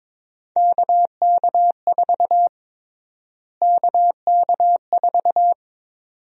Morse code, heard as a keyed tone, about 22 wpm, 700 Hz. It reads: KK4 KK4
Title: USA Call Sign Prefixes - Repeated 2x - 22wpm